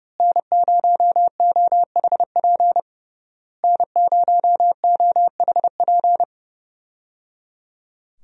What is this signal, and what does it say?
Morse code, heard as a keyed tone, about 30 wpm, 700 Hz. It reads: N0OHP N0OHP
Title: USA Call Signs - 5 Characters - Repeated 3x - 30wpm